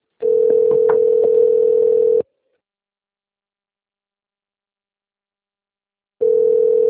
Ringback tone (North America), 440 + 480 Hz.